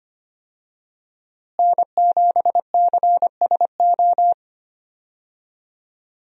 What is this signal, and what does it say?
Morse code, heard as a keyed tone, about 25 wpm, 700 Hz. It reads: N7CSO
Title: USA Call Signs - 5 Characters - Repeated 3x - 25wpm